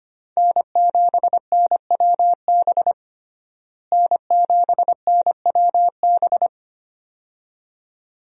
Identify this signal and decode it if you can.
Morse code, heard as a keyed tone, about 25 wpm, 700 Hz. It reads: N7NWB N7NWB
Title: USA Call Signs - 5 Characters - Repeated 2x - 25wpm